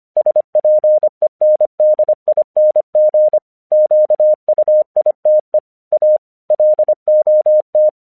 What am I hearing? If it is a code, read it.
Morse code, heard as a keyed tone, about 25 wpm, 610 Hz. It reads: SPENDING QUITE A LOT
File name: 1 English 25wpm / AQC08